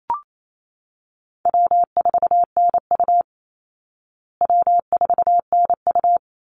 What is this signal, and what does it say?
Morse code, heard as a keyed tone, about 28 wpm, 700 Hz. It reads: W4NU W4NU
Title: Easy Call Signs Repeated 2x - 28wpm